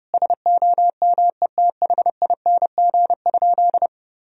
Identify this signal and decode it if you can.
Morse code, heard as a keyed tone, about 30 wpm, 700 Hz. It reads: SOMETHING?
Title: Sentences from Top 1000 Words - Review Part 2 - 30wpm